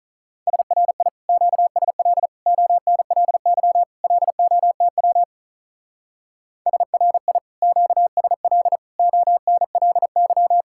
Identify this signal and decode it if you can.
Morse code, heard as a keyed tone, about 44 wpm, 700 Hz. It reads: SRI QSL ONLY LOTW SRI QSL ONLY
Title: Speed-Racing Common QSO Phrases Encore - 35wpm